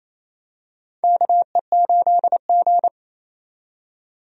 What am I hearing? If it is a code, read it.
Morse code, heard as a keyed tone, about 28 wpm, 700 Hz. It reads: KE8G
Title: USA Call Signs - 4 Characters - 28wpm